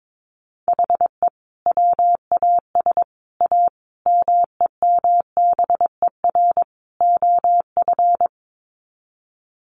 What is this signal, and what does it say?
Morse code, heard as a keyed tone, about 22 wpm, 700 Hz. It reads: HE WAS A MEMBER OF
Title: Sets of 5 Words Encore2 22wpm